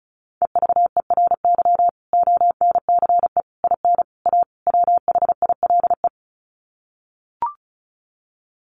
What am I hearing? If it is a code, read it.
Morse code, heard as a keyed tone, about 34 wpm, 700 Hz. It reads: EVERY ONCE IN A WHILE
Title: Sets of 5 Words 35wpm